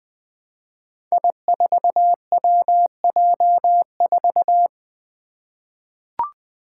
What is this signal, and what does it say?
Morse code, heard as a keyed tone, about 20 wpm, 700 Hz. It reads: I4WJ4
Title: Five Challenging Letter-Number Combinations 20wpm